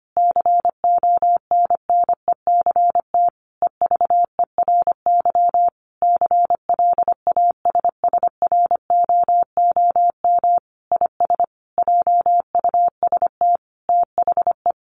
Morse code, heard as a keyed tone, about 25 wpm, 700 Hz. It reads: CONNECT EVERY CLASSROOM IS JUST THE
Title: Sentences from Top 1500 Words 25wpm